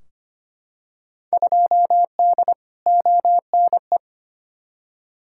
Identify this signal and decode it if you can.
Morse code, heard as a keyed tone, about 25 wpm, 700 Hz. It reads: 2D ONE
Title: Field Day Exchanges - 25wpm